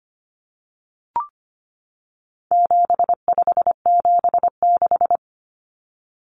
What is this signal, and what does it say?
Morse code, heard as a keyed tone, about 25 wpm, 700 Hz. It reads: 7576
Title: Common Cut Numbers - 4-Digit 25wpm